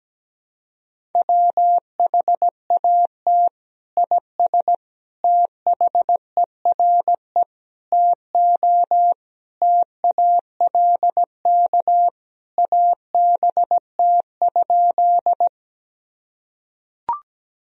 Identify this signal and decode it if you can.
Morse code, heard as a keyed tone, about 17 wpm, 700 Hz. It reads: WHAT IS THERE TO TALK ABT?
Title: Sentences from Top 400 Words and Abbreviations 17wpm